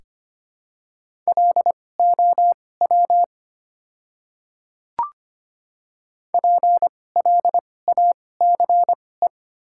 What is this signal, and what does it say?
Morse code, heard as a keyed tone, about 25 wpm, spaced as for 18 wpm, 700 Hz. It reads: LOW PLACE
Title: 2x Character Spacing - Top 200 Words - 25wpm